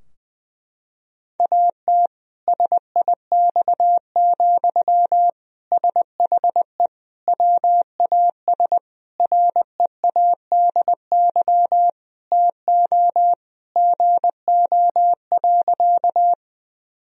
Morse code, heard as a keyed tone, about 20 wpm, 700 Hz. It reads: AT SIX, SHE WAS READY TO GO.
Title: Sentences from Top 1000 Words - Review Part 2 - 20wpm